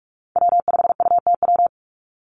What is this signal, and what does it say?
Morse code, heard as a keyed tone, about 46 wpm, 700 Hz. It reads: W5UTW
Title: Call Signs - 45wpm